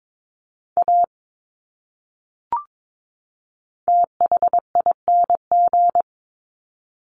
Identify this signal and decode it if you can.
Morse code, heard as a keyed tone, about 22 wpm, 700 Hz. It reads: A THING
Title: Top 200 Words 22wpm